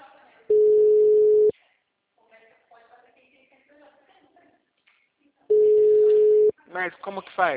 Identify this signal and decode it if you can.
Ringback tone (Europe, 425 Hz), with speech.
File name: out- -2011- - - .6488